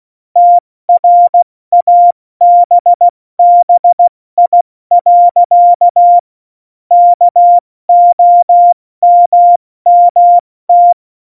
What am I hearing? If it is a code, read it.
Morse code, heard as a keyed tone, about 16 wpm, spaced as for 14 wpm, 700 Hz. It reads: TRABBI. KOMMT
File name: Morsekurs / Witze / Witz 05. Februar 2019, 70 F80